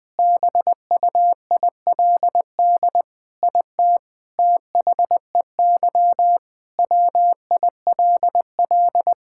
Morse code, heard as a keyed tone, about 20 wpm, 700 Hz. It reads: BUILD IT THEY WILL
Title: Sentences with 7-Word Phrases - 20wpm